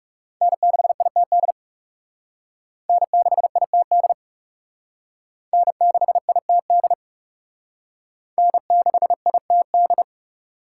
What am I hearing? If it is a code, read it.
Morse code, heard as a keyed tone, about 36 wpm, 700 Hz. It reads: N6ITD N6ITD N6ITD N6ITD
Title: Speed Racing - USA 1x3 Call Signs 30wpm